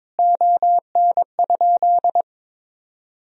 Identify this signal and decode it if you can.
Morse code, heard as a keyed tone, about 22 wpm, 700 Hz. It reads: ON?
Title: Sentences from Top 1000 Words - Review Part 3 - 22wpm